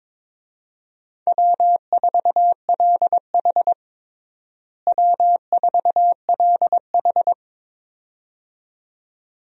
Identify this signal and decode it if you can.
Morse code, heard as a keyed tone, about 22 wpm, 700 Hz. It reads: W4LH W4LH
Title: USA Call Signs - 4 Characters - Repeated 2x - 22wpm